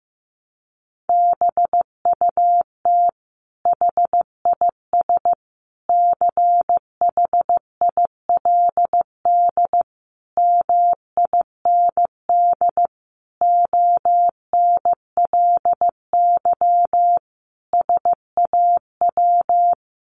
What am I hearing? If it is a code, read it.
Morse code, heard as a keyed tone, about 15 wpm, 700 Hz. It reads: BUT HIS CHILD MIND ONLY SAW
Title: Sentences from Top 800 Words 15wpm